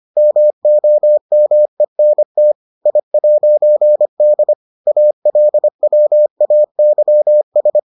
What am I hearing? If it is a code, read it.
Morse code, heard as a keyed tone, about 25 wpm, 600 Hz. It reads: MOMENT I'D ALWAYS